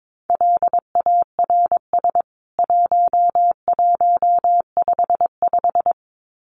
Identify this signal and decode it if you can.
Morse code, heard as a keyed tone, about 22 wpm, 700 Hz. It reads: LARS 1155
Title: CWT Exchange 22wpm